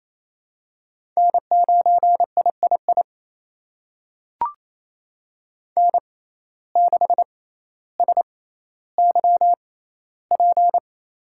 Morse code, heard as a keyed tone, about 28 wpm, spaced as for 14 wpm, 700 Hz. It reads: N9III N6SYP
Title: USA Call Signs - 5 Characters - 6x Character Spacing - 28wpm